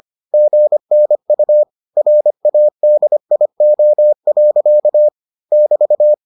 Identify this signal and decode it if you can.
Morse code, heard as a keyed tone, about 25 wpm, 600 Hz. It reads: GNU RADIO. <BT>